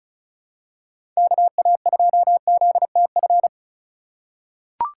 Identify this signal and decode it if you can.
Morse code, heard as a keyed tone, about 36 wpm, 700 Hz. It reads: KA2ZTF
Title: USA Call Signs - 6 Characters - 4x Character Spacing - 35wpm